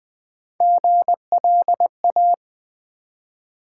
Morse code, heard as a keyed tone, about 20 wpm, 700 Hz. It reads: GLA
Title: Build A Word - 1 Letter At A Time - 4-Letter Words - 20wpm